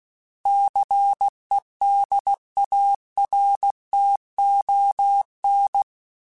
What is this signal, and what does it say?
Morse code, heard as a keyed tone, about 16 wpm, 800 Hz. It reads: CEDARTON
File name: fromSMs / FromSMs mp3s / Cities@15